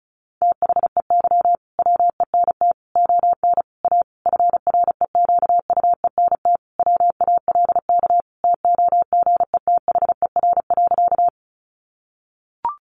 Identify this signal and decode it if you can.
Morse code, heard as a keyed tone, about 34 wpm, 700 Hz. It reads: THEY WENT ON A FREQUENT WALK TOGETHER.
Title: Sentences from Top 1400 Words 35wpm